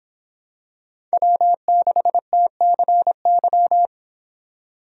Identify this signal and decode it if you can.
Morse code, heard as a keyed tone, about 25 wpm, 700 Hz. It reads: W6TCY